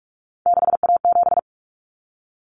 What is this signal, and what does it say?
Morse code, heard as a keyed tone, about 44 wpm, 700 Hz. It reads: 6A7
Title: Common Cut Numbers - 3-Digit 45wpm